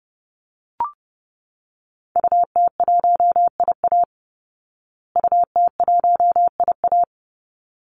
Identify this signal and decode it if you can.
Morse code, heard as a keyed tone, about 28 wpm, 700 Hz. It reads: UT1IA UT1IA
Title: Call Signs Repeated 3x - 30wpm